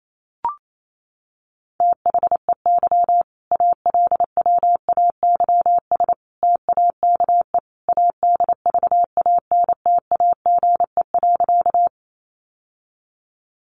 Morse code, heard as a keyed tone, about 28 wpm, 700 Hz. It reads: THEY ALWAYS TAKE ADVANTAGE.
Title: Sentences from Top 900 Words 28wpm